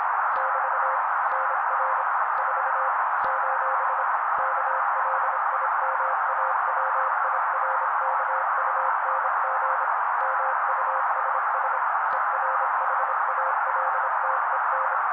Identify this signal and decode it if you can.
Morse code, heard as a keyed tone, about 26 wpm, 530 Hz. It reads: <BT> NR 4 : KLIMAWIRKUNG MUSS ERHALTEN
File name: daily / 25wpm noise